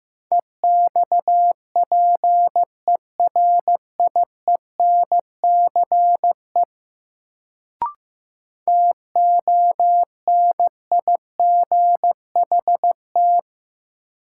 Morse code, heard as a keyed tone, about 15 wpm, 700 Hz. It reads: EXPERIENCE TONIGHT
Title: Sentences from Top 600 Words - New Word Drill - 15wpm